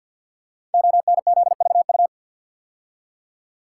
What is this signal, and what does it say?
Morse code, heard as a keyed tone, about 52 wpm, 700 Hz. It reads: ONZVU
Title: Code Groups - 5 Letter:Numbers - 50wpm